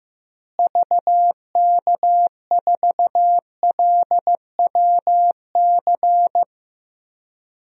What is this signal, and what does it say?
Morse code, heard as a keyed tone, about 15 wpm, 700 Hz. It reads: VK4LWC
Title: DXCC-100 Call Signs - 6 Characters - Repeated 3x - 15wpm